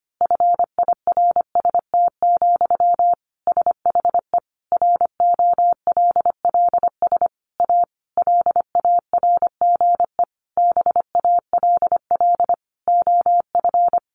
Morse code, heard as a keyed tone, about 25 wpm, 700 Hz. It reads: FIRST, SHE ROLLS A LARGE BALL OF
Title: Easy Phrases 25wpm